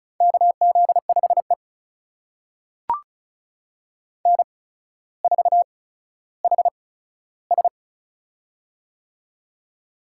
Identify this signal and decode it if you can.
Morse code, heard as a keyed tone, about 36 wpm, spaced as for 15 wpm, 700 Hz. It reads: KZ5E N4HS
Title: USA Call Signs - 4 Characters - 8x Character Spacing - 35wpm